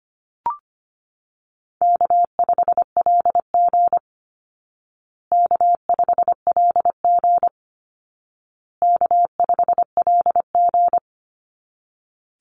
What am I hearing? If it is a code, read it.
Morse code, heard as a keyed tone, about 25 wpm, 700 Hz. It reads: K5LG K5LG K5LG
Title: Easy Call Signs Repeated 3x - 25wpm